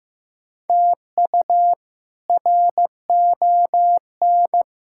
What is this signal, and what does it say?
Morse code, heard as a keyed tone, about 15 wpm, 700 Hz. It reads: TU RON
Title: K1USN SST QSO Practice - 15wpm